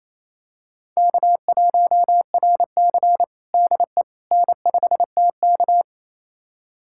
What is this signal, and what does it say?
Morse code, heard as a keyed tone, about 28 wpm, 700 Hz. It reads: K1RC DE N5TK
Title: USA Callsign DE Callsign - 4 Characters - 28wpm